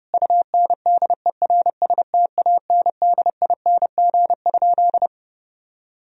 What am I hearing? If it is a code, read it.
Morse code, heard as a keyed tone, about 30 wpm, 700 Hz. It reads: UNDERSTANDING?
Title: Sentences from Top 1000 Words - Review Part 3 - 30wpm